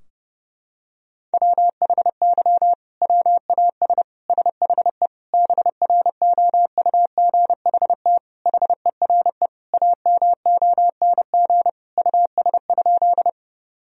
Morse code, heard as a keyed tone, about 30 wpm, 700 Hz. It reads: WHY WAS SHE BROUGHT HERE AMONG US?
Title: Sentences from Top 1000 Words - Review Part 3 - 30wpm